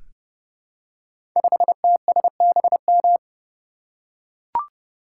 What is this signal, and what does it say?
Morse code, heard as a keyed tone, about 30 wpm, 700 Hz. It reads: 5TSBM
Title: Code Groups - 5 Letter:Numbers - 30wpm